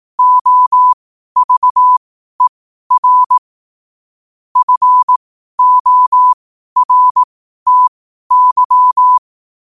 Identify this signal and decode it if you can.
Morse code, heard as a keyed tone, about 18 wpm, spaced as for 12 wpm, 1000 Hz. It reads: OVER FORTY
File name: qotdfiles / 12wpm